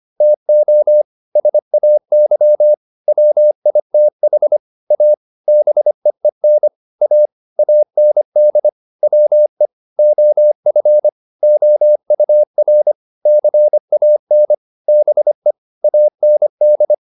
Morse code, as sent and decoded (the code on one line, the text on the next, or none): - --- / ... .- -.-- / .-- .. - .... / .- / -... . -. / .- / .- -. -.. / .-- . / --- ..-. / --- ..- .-. / -.-. .- -. / -... / .- -. -..
TO SAY WITH A BEEN A AND WE OF OUR CAN BE AND